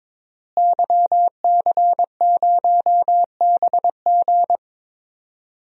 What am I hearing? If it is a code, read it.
Morse code, heard as a keyed tone, about 22 wpm, 700 Hz. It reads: YC0BG
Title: DXCC-100 Call Signs - 5 Characters - Repeated 2x - 22wpm